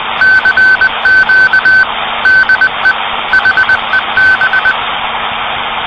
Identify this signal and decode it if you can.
Morse code, heard as a keyed tone, about 20 wpm, 1550 Hz, with noise. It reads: CQ DE HEB
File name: CW PACTOR CQ de HEB